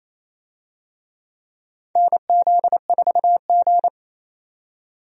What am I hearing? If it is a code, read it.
Morse code, heard as a keyed tone, about 28 wpm, 700 Hz. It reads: NZ4G
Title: USA Call Signs - 4 Characters - 28wpm